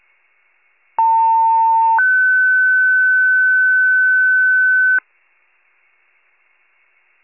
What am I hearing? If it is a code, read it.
Two-tone page, heard as a one-second tone, then a three-second tone, about 900 then 1500 Hz. It is heard from 1.0 to 5.0 s.
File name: pager / 94 RESCUE OFC / wcalert